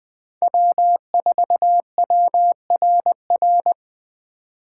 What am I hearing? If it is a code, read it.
Morse code, heard as a keyed tone, about 21 wpm, 700 Hz. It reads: W4WRR
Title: Speed Racing - USA 1x3 Call Signs 17wpm